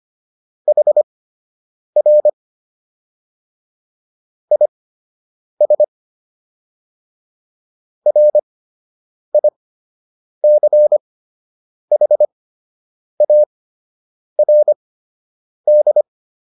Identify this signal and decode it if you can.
Morse code, heard as a keyed tone, about 25 wpm, spaced as for 8 wpm, 600 Hz. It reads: HR IS RICHARD